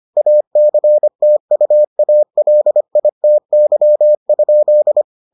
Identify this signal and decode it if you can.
Morse code, heard as a keyed tone, about 25 wpm, 600 Hz. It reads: ACTUALITY?